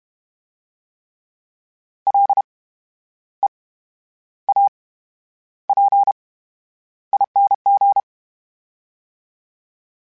Morse code, heard as a keyed tone, about 32 wpm, spaced as for 11 wpm, 800 Hz. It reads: LEAPING
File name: Special INGs / Special INGs-mp3s / VerySlowToFaster Ing @30